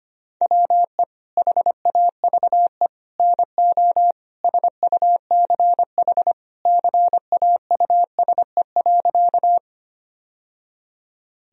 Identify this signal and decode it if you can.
Morse code, heard as a keyed tone, about 25 wpm, 700 Hz. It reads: WE HAVE NO SUCH CAUSE.
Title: Sentences from Top 1000 Words - Review Part 1 - 25wpm